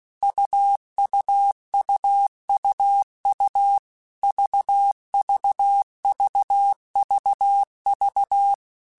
Morse code, heard as a keyed tone, about 16 wpm, 800 Hz. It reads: UUUUU VVVVV